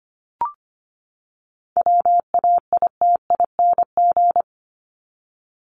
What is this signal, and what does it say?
Morse code, heard as a keyed tone, about 25 wpm, 700 Hz. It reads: WAITING
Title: ING Suffix - Top 100 Words with Base Words 25wpm